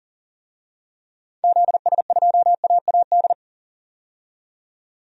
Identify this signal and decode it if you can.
Morse code, heard as a keyed tone, about 40 wpm, 700 Hz. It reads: ZS2AAD
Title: DXCC-100 Call Signs - 6 Characters - 40wpm